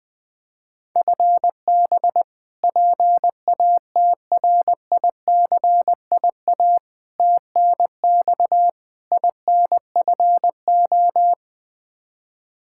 Morse code, heard as a keyed tone, about 20 wpm, 700 Hz. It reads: FB PATRICIA TNX INFO
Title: Practice QSO - Exchange 3 - More Info - 20wpm